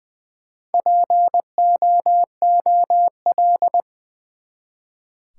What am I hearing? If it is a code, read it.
Morse code, heard as a keyed tone, about 20 wpm, 700 Hz. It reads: POOL
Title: Four Letter Words - 20wpm